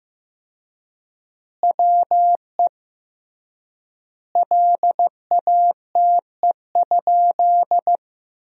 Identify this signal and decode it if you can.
Morse code, heard as a keyed tone, about 16 wpm, 700 Hz. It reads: WE LATE?
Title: 3x Word-Spacing - Sentences from Top 300 Words - 15wpm